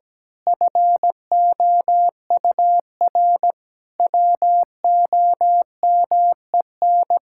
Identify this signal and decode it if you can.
Morse code, heard as a keyed tone, about 17 wpm, 700 Hz. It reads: FOUR WOMEN
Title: Pangram Challenge v2 - 17wpm